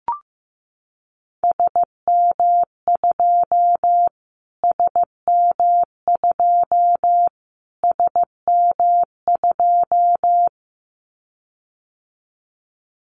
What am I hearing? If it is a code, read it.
Morse code, heard as a keyed tone, about 15 wpm, 700 Hz. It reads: SM2 SM2 SM2
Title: Callsign Prefixes for Top 25 DXCC - Repeated 3x - 15wpm